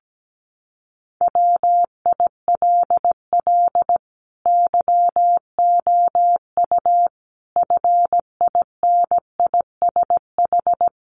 Morse code, heard as a keyed tone, about 17 wpm, 700 Hz. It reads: WILL YOU FINISH
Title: Sentences from Top 1300 Words 17wpm